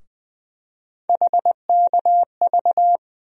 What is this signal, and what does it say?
Morse code, heard as a keyed tone, about 21 wpm, 700 Hz. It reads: HKV
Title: USA Call Sign Postfixes - 3 Letters 20wpm